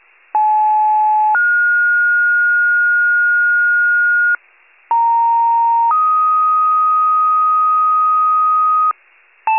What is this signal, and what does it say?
Two-tone page, heard as a one-second tone, then a three-second tone, about 860 then 1430 Hz, about 930 then 1250 Hz, with noise. It is heard from 0.3 to 8.9 s.
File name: pager / 82 FIRE / wcalert